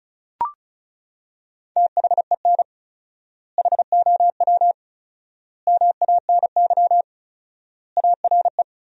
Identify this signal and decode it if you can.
Morse code, heard as a keyed tone, about 34 wpm, 700 Hz. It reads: THEN HOW MANY ARE
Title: 4x Word-Spacing - Sentences from Top 100 Words - 35wpm